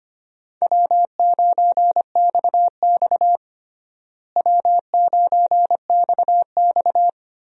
Morse code, heard as a keyed tone, about 25 wpm, 700 Hz. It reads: W9XX W9XX
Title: USA Call Signs - 4 Characters - Repeated 3x - 25wpm